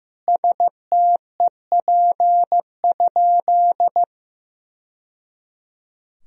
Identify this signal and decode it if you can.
Morse code, heard as a keyed tone, about 15 wpm, 700 Hz. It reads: STEP?
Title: Sentences from Top 1000 Words - Review Part 2 - 15wpm